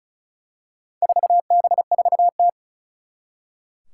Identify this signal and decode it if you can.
Morse code, heard as a keyed tone, about 36 wpm, 690 Hz. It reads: 4B4T
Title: Four Challenging Letter-Number Combinations 35wpm